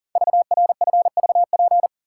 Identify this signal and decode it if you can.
Morse code, heard as a keyed tone, about 42 wpm, 700 Hz. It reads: VRFVP
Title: Code Groups - 5 Letters - 40wpm